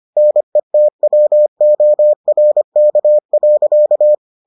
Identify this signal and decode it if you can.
Morse code, heard as a keyed tone, about 25 wpm, 600 Hz. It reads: NETWORK.